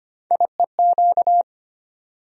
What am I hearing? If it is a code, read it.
Morse code, heard as a keyed tone, about 25 wpm, 700 Hz. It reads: IEQ
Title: USA Call Sign Postfixes - 3 Letters 25wpm